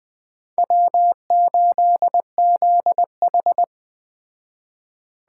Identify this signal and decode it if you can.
Morse code, heard as a keyed tone, about 20 wpm, 700 Hz. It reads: W8ZH
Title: Speed Racing - USA 1x2 Call Signs 17wpm